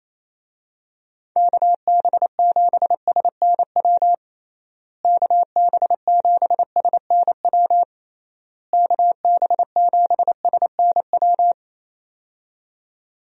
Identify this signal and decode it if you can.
Morse code, heard as a keyed tone, about 28 wpm, 700 Hz. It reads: KB7SNW KB7SNW KB7SNW
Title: USA Call Signs - 6 Characters - Repeated 3x - 28wpm